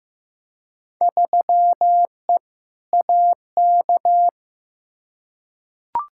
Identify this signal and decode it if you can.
Morse code, heard as a keyed tone, about 15 wpm, 700 Hz. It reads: 3E AK
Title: Field Day Exchanges Encore - 15wpm